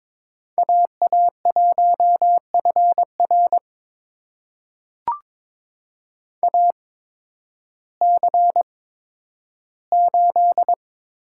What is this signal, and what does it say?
Morse code, heard as a keyed tone, about 22 wpm, spaced as for 12 wpm, 700 Hz. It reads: AA1FR AC8
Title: USA Call Signs - 5 Characters - 8x Character Spacing - 22wpm